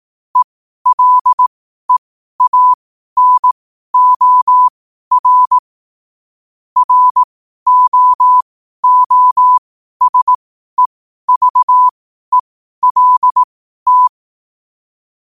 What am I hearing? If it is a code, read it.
Morse code, heard as a keyed tone, about 18 wpm, spaced as for 12 wpm, 1000 Hz. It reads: ELEANOR ROOSEVELT
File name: qotdfiles / 12wpm